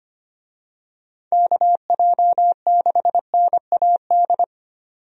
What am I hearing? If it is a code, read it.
Morse code, heard as a keyed tone, about 25 wpm, 700 Hz. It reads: KJ6NAD
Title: USA Call Signs - 6 Characters - 25wpm